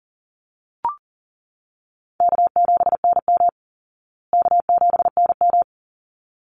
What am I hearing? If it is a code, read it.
Morse code, heard as a keyed tone, about 40 wpm, 700 Hz. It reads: K7NM K7NM
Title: Call Signs Repeated 3x - 40wpm